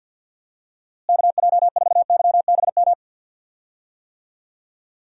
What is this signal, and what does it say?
Morse code, heard as a keyed tone, about 51 wpm, 700 Hz. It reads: KJ4YBM
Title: USA Call Signs - 6 Characters - Repeated 2x - 50wpm